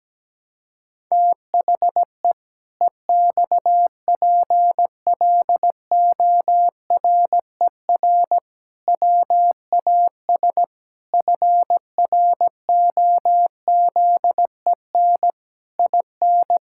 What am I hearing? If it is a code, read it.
Morse code, heard as a keyed tone, about 17 wpm, 700 Hz. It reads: THE EXPLORER WAS FROZEN IN
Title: Pangram Challenge v2 - 17wpm